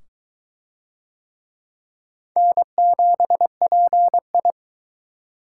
Morse code, heard as a keyed tone, about 22 wpm, 700 Hz. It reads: N7PI